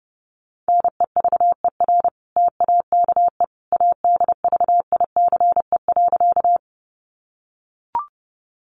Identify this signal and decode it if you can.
Morse code, heard as a keyed tone, about 30 wpm, 700 Hz. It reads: NEVER TAKE ADVICE.
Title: Sentences from Top 1100 Words 30wpm